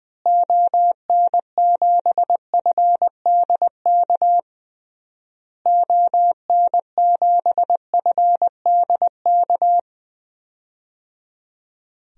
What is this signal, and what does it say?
Morse code, heard as a keyed tone, about 20 wpm, 700 Hz. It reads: ON7FDK ON7FDK
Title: DXCC-100 Call Signs - 6 Characters - Repeated 3x - 20wpm